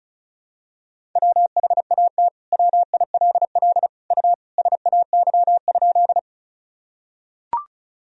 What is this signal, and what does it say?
Morse code, heard as a keyed tone, about 35 wpm, 700 Hz. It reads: WHAT WILL U SAY?
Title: Sentences from Top 200 Words and Abbreviations 35wpm